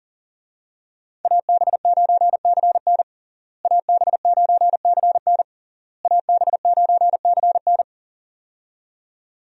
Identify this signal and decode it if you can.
Morse code, heard as a keyed tone, about 40 wpm, 700 Hz. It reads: AB9CN AB9CN AB9CN
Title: USA Call Signs - 5 Characters - Repeated 3x - 40wpm